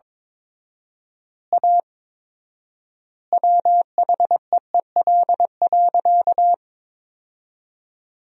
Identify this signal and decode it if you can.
Morse code, heard as a keyed tone, about 22 wpm, 700 Hz. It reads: A WHEEL.
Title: 4x Word-Spacing - Sentences from Top 500 Words - 22wpm